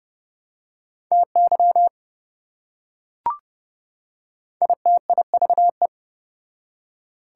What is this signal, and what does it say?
Morse code, heard as a keyed tone, about 28 wpm, 700 Hz. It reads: TY ITIVE